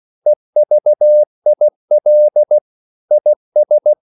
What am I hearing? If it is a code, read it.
Morse code, heard as a keyed tone, about 16 wpm, 600 Hz. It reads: EVIL IS
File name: CW / Practice / EnglishQuotations0416